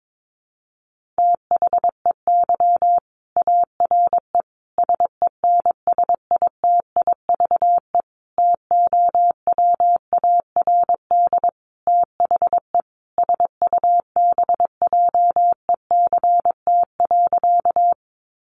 Morse code, heard as a keyed tone, about 22 wpm, 700 Hz. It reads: THEY ARE SENSITIVE TOWARD THE SUBJECT.
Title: Sentences from Top 1400 Words 22wpm